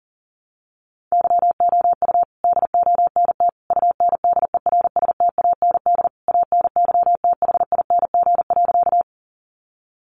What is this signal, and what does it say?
Morse code, heard as a keyed tone, about 37 wpm, 700 Hz. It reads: YOU DONT UNDERSTAND ANYTHING.
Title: Sentences from Top 1000 Words 40wpm